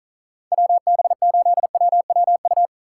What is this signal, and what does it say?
Morse code, heard as a keyed tone, about 42 wpm, 700 Hz. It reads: WB8WWU
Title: Speed Racing - USA 2x3 Call Signs 35wpm